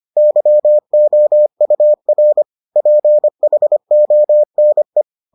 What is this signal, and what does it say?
Morse code, heard as a keyed tone, about 25 wpm, 600 Hz. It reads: YOUR PHONE